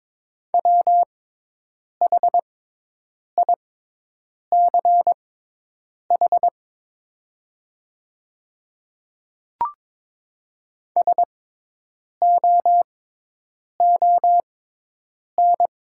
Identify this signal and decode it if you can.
Morse code, heard as a keyed tone, about 22 wpm, spaced as for 8 wpm, 700 Hz. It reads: WHICH SOON
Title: 6x Character Spacing - Top 500 Words - 22wpm